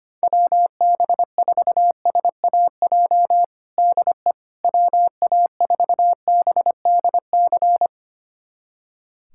Morse code, heard as a keyed tone, about 25 wpm, 700 Hz. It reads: WB4SAJ DE WA4BDC
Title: USA Callsign DE Callsign - 6 Characters - 25wpm